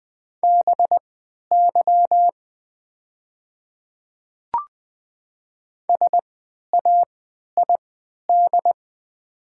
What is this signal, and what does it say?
Morse code, heard as a keyed tone, about 20 wpm, spaced as for 11 wpm, 700 Hz. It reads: BY SAID
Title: 3x Character Spacing - Top 100 Words - 20wpm